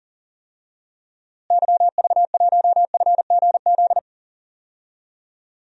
Morse code, heard as a keyed tone, about 37 wpm, 700 Hz. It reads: YV1FGZ
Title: DXCC-100 Call Signs - 6 Characters - Repeated 3x - 40wpm